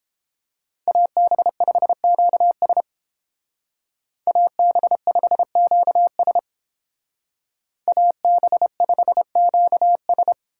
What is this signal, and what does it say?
Morse code, heard as a keyed tone, about 30 wpm, 700 Hz. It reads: AB5QS AB5QS AB5QS